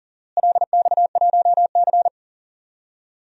Morse code, heard as a keyed tone, about 40 wpm, 700 Hz. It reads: LX1C
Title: DXCC-100 Call Signs - 4 Characters - Repeated 3x - 40wpm